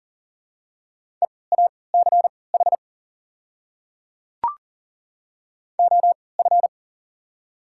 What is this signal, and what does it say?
Morse code, heard as a keyed tone, about 38 wpm, spaced as for 22 wpm, 700 Hz. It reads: EACH OF
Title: 3x Character Spacing - Top 100 Words - 40wpm